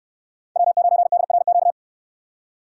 Morse code, heard as a keyed tone, about 64 wpm, 700 Hz. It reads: W9DRQ